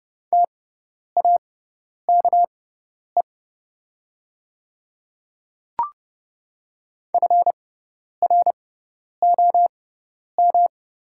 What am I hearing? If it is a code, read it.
Morse code, heard as a keyed tone, about 30 wpm, spaced as for 10 wpm, 700 Hz. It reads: TAKE FROM
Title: 6x Character Spacing - Top 300 Words - 30wpm